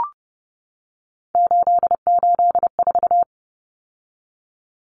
Morse code, heard as a keyed tone, about 30 wpm, 700 Hz. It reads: 884
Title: Common Cut Numbers - 3-Digit 30wpm